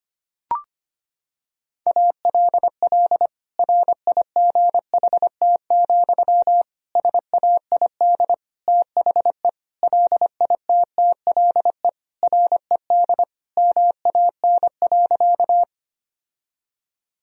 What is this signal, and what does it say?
Morse code, heard as a keyed tone, about 25 wpm, 700 Hz. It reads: ALL RIGHT, SAID THE LITTLE RED MAN.
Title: Sentences from Top 1000 Words - Review Part 2 - 25wpm